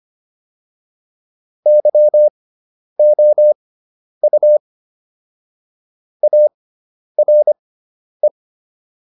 Morse code, heard as a keyed tone, about 25 wpm, spaced as for 10 wpm, 600 Hz. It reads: YOU ARE